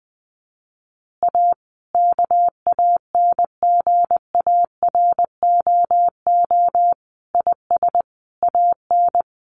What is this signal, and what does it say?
Morse code, heard as a keyed tone, about 20 wpm, 700 Hz. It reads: A KANGAROO IS AN
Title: Easy Phrases 20wpm